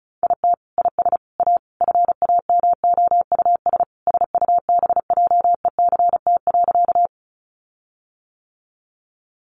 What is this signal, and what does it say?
Morse code, heard as a keyed tone, about 34 wpm, 700 Hz. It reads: IT IS A FAMOUS SUBJECT.
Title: Sentences from Top 1500 Words 35wpm